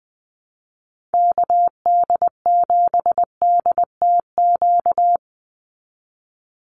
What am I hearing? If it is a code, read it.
Morse code, heard as a keyed tone, about 20 wpm, 700 Hz. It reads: KD7DTQ
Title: USA Call Signs 20wpm